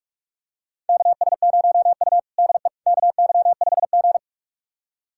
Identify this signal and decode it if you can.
Morse code, heard as a keyed tone, about 41 wpm, 700 Hz. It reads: KS0U DE KY5G
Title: USA Callsign DE Callsign - 4 Characters - 45wpm